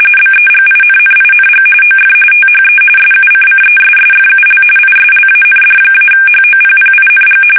RTTY, about 45.45 baud. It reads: BEGINNING OF THE SUCCESSFUL ISLAND-HOPPING
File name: fsk850 / battleshipnj